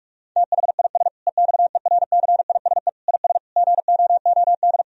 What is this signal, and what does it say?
Morse code, heard as a keyed tone, about 41 wpm, 700 Hz. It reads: THIS EXERCISE IS GOOD
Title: Sentences from Top 1000 Words - Review Part 4 - 45wpm